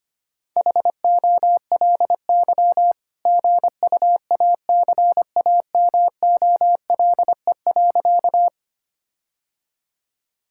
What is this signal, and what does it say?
Morse code, heard as a keyed tone, about 25 wpm, 700 Hz. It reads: HOLY GUACAMOLE.
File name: practice / 25wpm / All-Age Appropriate Jokes - 25wpm